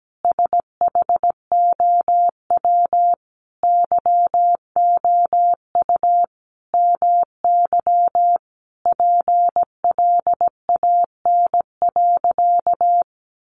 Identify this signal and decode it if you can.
Morse code, heard as a keyed tone, about 17 wpm, 700 Hz. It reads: SHOW YOU MY PLAN.
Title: Sentences from Top 500 Words 17wpm